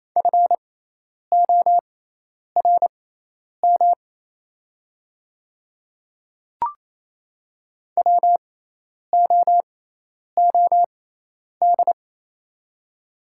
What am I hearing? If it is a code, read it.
Morse code, heard as a keyed tone, about 28 wpm, spaced as for 10 wpm, 690 Hz. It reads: FORM WOOD
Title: 6x Character Spacing - Top 500 Words - 28wpm